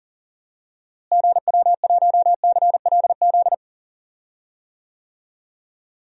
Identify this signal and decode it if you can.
Morse code, heard as a keyed tone, about 40 wpm, 700 Hz. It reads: GW1CLZ
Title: DXCC-100 Call Signs - 6 Characters - Repeated 3x - 40wpm